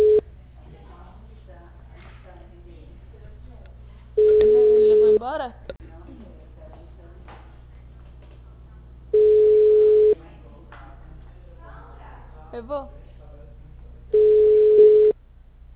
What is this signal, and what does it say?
Ringback tone (Europe, 425 Hz), with speech.
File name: out- -2003- - - .30427